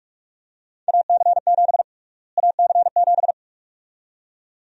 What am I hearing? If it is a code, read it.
Morse code, heard as a keyed tone, about 45 wpm, 700 Hz. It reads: AC7 AC7
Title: USA Call Sign Prefixes - Repeated 2x - 45wpm